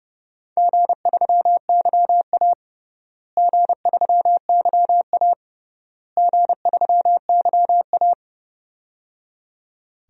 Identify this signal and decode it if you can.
Morse code, heard as a keyed tone, about 30 wpm, 700 Hz. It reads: G3YA G3YA G3YA
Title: DXCC-100 Call Signs - 4 Characters - Repeated 3x - 30wpm